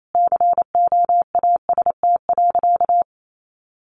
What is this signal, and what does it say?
Morse code, heard as a keyed tone, about 28 wpm, 700 Hz. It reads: COAST.
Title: Sentences from Top 1300 Words 28wpm